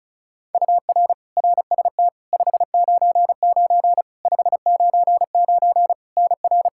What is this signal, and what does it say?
Morse code, heard as a keyed tone, about 36 wpm, 700 Hz. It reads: UR RST 599 599 NR
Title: Practice QSO - Exchange 1 - Beginning - 35wpm